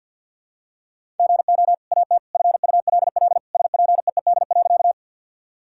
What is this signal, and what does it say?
Morse code, heard as a keyed tone, about 51 wpm, 700 Hz. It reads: GO AT FULL SPEED.
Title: Sentences from Top 1000 Words - Review Part 5 - 50wpm